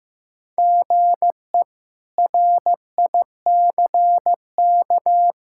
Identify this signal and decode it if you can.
Morse code, heard as a keyed tone, about 15 wpm, 700 Hz. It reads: GE RICK
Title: K1USN SST QSO Practice - 15wpm